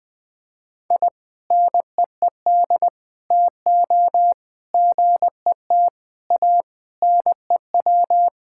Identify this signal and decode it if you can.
Morse code, heard as a keyed tone, about 21 wpm, 700 Hz. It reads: I NEED TO GET A NEW
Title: Sentences with 3-Word Phrases - 20wpm